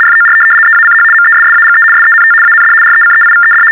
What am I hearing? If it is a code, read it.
RTTY, about 75 baud. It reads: FOR EACH MESSAGE SENT AND RECEIVED
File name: teletype-RTTY-75